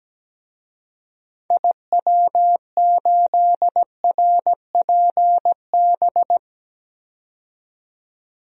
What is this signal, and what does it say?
Morse code, heard as a keyed tone, about 17 wpm, 700 Hz. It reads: IW8RPB
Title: DXCC-100 Call Signs - 6 Characters - 17wpm